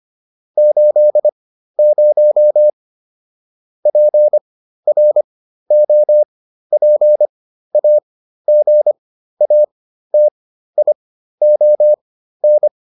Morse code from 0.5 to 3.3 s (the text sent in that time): 80